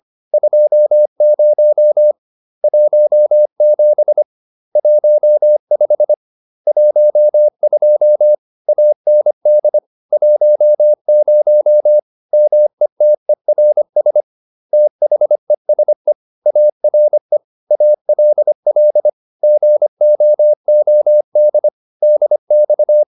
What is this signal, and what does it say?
Morse code, heard as a keyed tone, about 25 wpm, 600 Hz. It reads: 20 17 15 12 AND 10 METERS THESE ARE ALL GOOD DX
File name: SS311 25